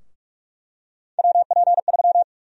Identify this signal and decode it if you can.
Morse code, heard as a keyed tone, about 46 wpm, 700 Hz. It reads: WP3
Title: USA Call Sign Prefixes - Repeated 2x - 45wpm